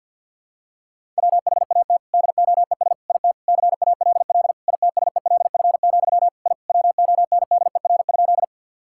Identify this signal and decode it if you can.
Morse code, heard as a keyed tone, about 48 wpm, 700 Hz. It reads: WHAT DOES IT CALL ITSELF, I WONDER?
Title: Sentences from Top 1000 Words - Review Part 5 - 50wpm